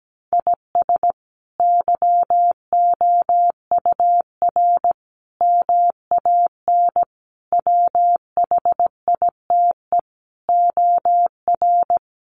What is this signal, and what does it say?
Morse code, heard as a keyed tone, about 17 wpm, 700 Hz. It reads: IS YOUR MAN WHITE OR
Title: Sentences from Top 400 Words 17wpm